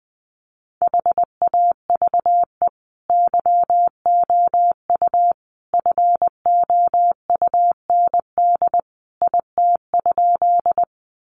Morse code, heard as a keyed tone, about 20 wpm, 700 Hz. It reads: HAVE YOU FOUND IT?
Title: Sentences from Top 300 Words 20wpm